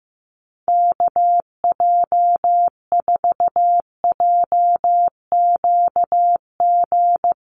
Morse code, heard as a keyed tone, about 15 wpm, 700 Hz. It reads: KJ4JQG
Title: USA Call Signs Repeated 3x 15wpm